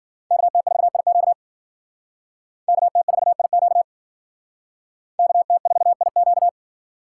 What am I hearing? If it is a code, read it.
Morse code, heard as a keyed tone, about 51 wpm, 700 Hz. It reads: KT4IQ KT4IQ KT4IQ